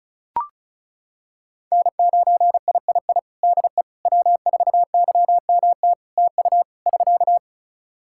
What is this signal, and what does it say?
Morse code, heard as a keyed tone, about 36 wpm, 700 Hz. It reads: N9III DE W4YMT TU <SK>
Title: Practice QSO - Exchange 4 - Ending - 35wpm